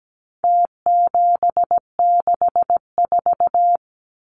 Morse code, heard as a keyed tone, about 17 wpm, 700 Hz. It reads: T764
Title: Numbers - Contest Style - 4-Digit 17wpm